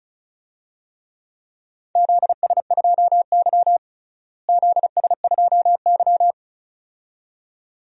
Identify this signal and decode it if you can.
Morse code, heard as a keyed tone, about 34 wpm, 700 Hz. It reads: ZS2Y ZS2Y
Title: DXCC-100 Call Signs - 4 Characters - Repeated 2x - 35wpm